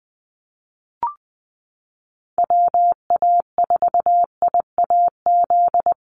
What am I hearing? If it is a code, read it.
Morse code, heard as a keyed tone, about 20 wpm, 700 Hz. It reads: WA4IAZ
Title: USA Call Signs Repeated 3x 20wpm